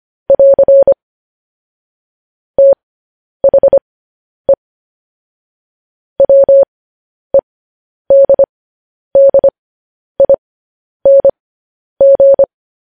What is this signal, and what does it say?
Morse code, heard as a keyed tone, about 25 wpm, spaced as for 10 wpm, 550 Hz. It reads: <AR> THE WEDDING
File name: podcast / News headlines-1x-25x10